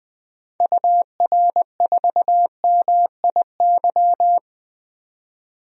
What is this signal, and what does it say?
Morse code, heard as a keyed tone, about 20 wpm, 700 Hz. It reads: UR4MIY
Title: DXCC-100 Call Signs - 6 Characters - 20wpm